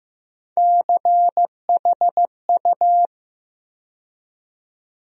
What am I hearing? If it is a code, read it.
Morse code, heard as a keyed tone, about 15 wpm, 700 Hz. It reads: CHU
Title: USA Call Sign Postfixes - 3 Letters 15wpm